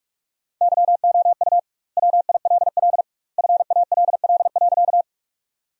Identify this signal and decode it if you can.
Morse code, heard as a keyed tone, about 44 wpm, 700 Hz. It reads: YOU WILL FALL.
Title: 2x Word-Spacing - Sentences from Top 500 Words - 45wpm